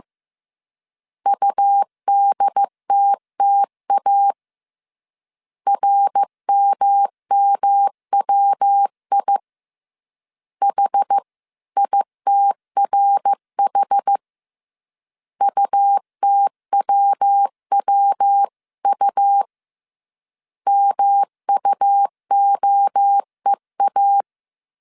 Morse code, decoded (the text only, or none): UDTTA RMMWI H ITRH UTWWU MUOEA